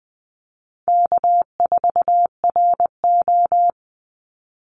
Morse code, heard as a keyed tone, about 20 wpm, 700 Hz. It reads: K4RO
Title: Call Signs - 20wpm